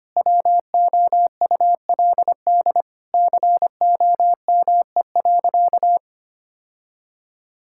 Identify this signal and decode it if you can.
Morse code, heard as a keyed tone, about 25 wpm, 700 Hz. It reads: WOULD COME.
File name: practice / 25wpm / Sentences with 2-Word Phrases - 25wpm